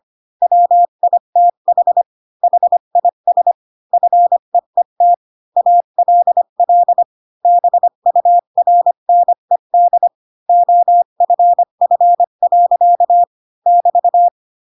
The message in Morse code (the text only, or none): WITH HIS FEET ALL BURNED OFF. <BT>